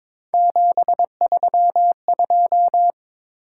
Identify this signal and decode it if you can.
Morse code, heard as a keyed tone, about 23 wpm, 700 Hz. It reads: 732
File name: practice / 22wpm / Three Letter-Number Combinations - 22wpm